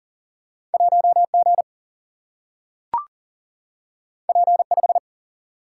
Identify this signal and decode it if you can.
Morse code, heard as a keyed tone, about 40 wpm, 700 Hz. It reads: 1G P5
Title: Single-Letter Single-Number 40wpm